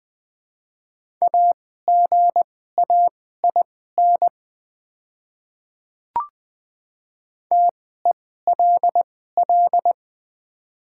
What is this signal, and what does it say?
Morse code, heard as a keyed tone, about 20 wpm, spaced as for 14 wpm, 700 Hz. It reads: AGAIN TELL
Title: 2x Character Spacing - Top 200 Words - 20wpm